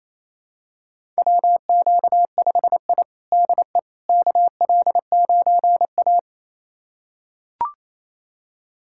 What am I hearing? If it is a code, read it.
Morse code, heard as a keyed tone, about 28 wpm, 700 Hz. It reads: WQ5I DE KL9A
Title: USA Callsign DE Callsign - 4 Characters - 28wpm